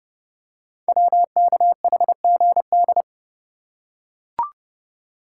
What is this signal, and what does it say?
Morse code, heard as a keyed tone, about 30 wpm, 700 Hz. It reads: WKHGD
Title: Code Groups - 5 Letters - 30wpm